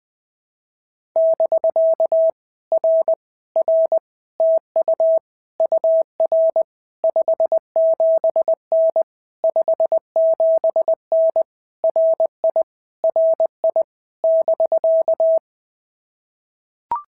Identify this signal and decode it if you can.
Morse code, heard as a keyed tone, about 20 wpm, 650 Hz. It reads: <BK> R R TU UR 57N 57N RI RI <BK>
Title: POTA QSO Practice - 20wpm